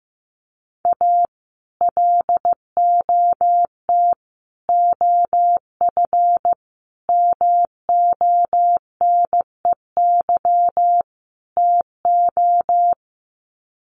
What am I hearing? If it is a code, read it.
Morse code, heard as a keyed tone, about 15 wpm, 700 Hz. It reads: A LOT OF MONEY TO
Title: Sets of 5 Words Encore2 15wpm